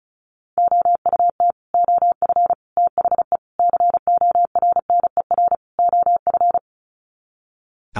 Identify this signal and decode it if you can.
Morse code, heard as a keyed tone, about 34 wpm, 700 Hz. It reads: OUT OF THE CORNER OF
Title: Sets of 5 Words 35wpm